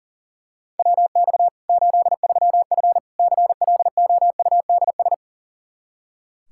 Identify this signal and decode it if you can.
Morse code, heard as a keyed tone, about 40 wpm, 700 Hz. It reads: WX 83F CLOUDS
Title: Practice QSO - Exchange 2 - Info - 40wpm